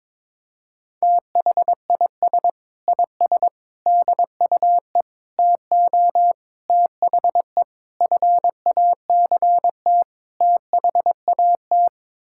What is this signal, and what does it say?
Morse code, heard as a keyed tone, about 22 wpm, 700 Hz. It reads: THIS IS DUE TO THE FACT THAT
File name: practice / 22wpm / Sets of 7 Words - 22wpm